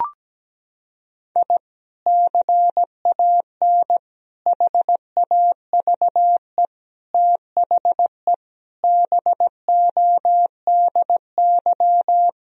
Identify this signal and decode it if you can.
Morse code, heard as a keyed tone, about 17 wpm, 700 Hz. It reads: I CAN HAVE THE BODY
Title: Sentences from Top 1000 Words - Review Part 2 - 17wpm